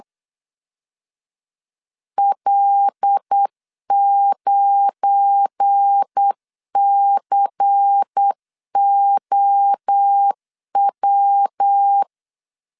Morse code, heard as a keyed tone, about 8 wpm, 800 Hz. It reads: L9COW
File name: LEZIONI / 7 GF9Z / 7 GF9 8wpm 3545